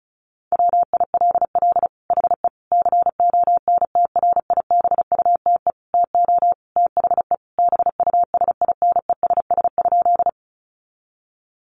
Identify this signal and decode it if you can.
Morse code, heard as a keyed tone, about 34 wpm, 700 Hz. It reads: WILL HE CONTRIBUTE TO THE BUSINESS?
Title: Sentences from Top 1400 Words 35wpm